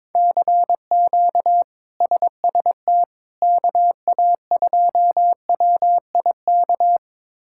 Morse code, heard as a keyed tone, about 22 wpm, 700 Hz. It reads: CQ SST KA2WIK
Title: K1USN SST QSO Practice - 22wpm